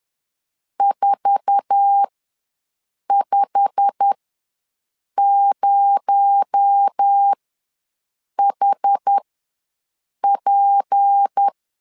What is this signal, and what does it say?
Morse code, heard as a keyed tone, about 10 wpm, spaced as for 6 wpm, 800 Hz. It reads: 450HP
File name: LEZIONI / 8 YJ45 / 8 YJ45 ws 10wpm 3843